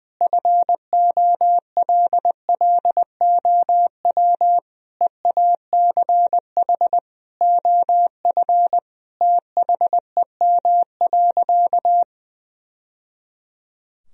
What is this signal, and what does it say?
Morse code, heard as a keyed tone, about 20 wpm, 700 Hz. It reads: FOLLOW EACH OF THEM.
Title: Sentences from Top 1000 Words - Review Part 1 - 20wpm